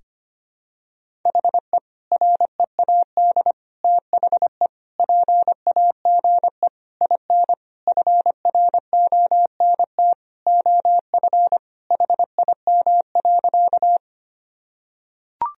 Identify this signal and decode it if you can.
Morse code, heard as a keyed tone, about 25 wpm, 700 Hz. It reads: HE READ THE PAGE IN FRONT OF HIM.
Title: Sentences from Top 1000 Words - Review Part 1 - 25wpm